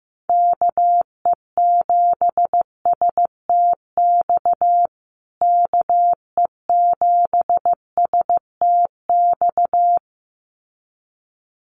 Morse code, heard as a keyed tone, about 15 wpm, 700 Hz. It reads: KE7STX KE7STX
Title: USA Call Signs Repeated 2x 15wpm